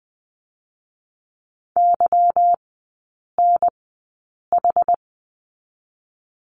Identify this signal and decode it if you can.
Morse code, heard as a keyed tone, about 20 wpm, 690 Hz. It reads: Y N H
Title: 145 Lesson 24e - Letter y - Three Characters 20wpm